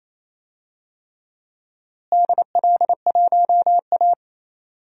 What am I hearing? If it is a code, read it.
Morse code, heard as a keyed tone, about 28 wpm, 700 Hz. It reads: DL1A
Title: DXCC-100 Call Signs - 4 Characters - Repeated 3x - 28wpm